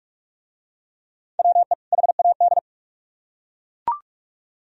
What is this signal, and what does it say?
Morse code, heard as a keyed tone, about 42 wpm, 700 Hz. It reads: WE HAD